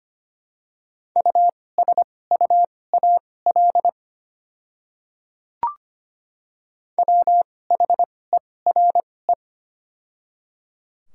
Morse code, heard as a keyed tone, about 25 wpm, spaced as for 18 wpm, 700 Hz. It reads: USUAL WHERE
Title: 2x Character Spacing - Top 500 Words - 25wpm